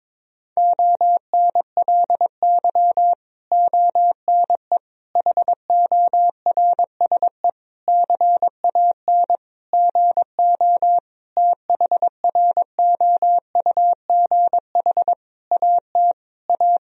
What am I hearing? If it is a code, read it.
Morse code, heard as a keyed tone, about 22 wpm, 700 Hz. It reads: ONLY ONE HORSE CAN GO THROUGH AT A
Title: Sentences from Top 1000 Words - Review Part 1 - 22wpm